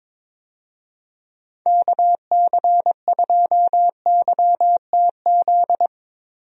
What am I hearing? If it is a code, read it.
Morse code, heard as a keyed tone, about 22 wpm, 700 Hz. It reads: KC2YTZ
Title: USA Call Signs - 6 Characters - Repeated 2x - 22wpm